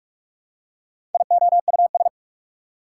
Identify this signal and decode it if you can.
Morse code, heard as a keyed tone, about 44 wpm, 690 Hz. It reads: IOUS
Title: Prefixes and Suffixes 45wpm